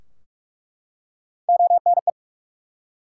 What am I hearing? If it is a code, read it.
Morse code, heard as a keyed tone, about 44 wpm, 700 Hz. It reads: ONE
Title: Three Letter Words - 45wpm